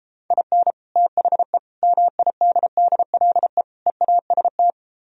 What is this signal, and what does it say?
Morse code, heard as a keyed tone, about 35 wpm, 700 Hz. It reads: IN THE MIDDLE EAST